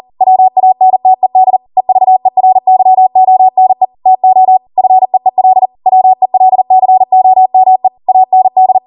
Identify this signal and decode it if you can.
Morse code, heard as a keyed tone, about 40 wpm, 750 Hz. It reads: WANTED EVERYONE TO FEEL WELCOME AND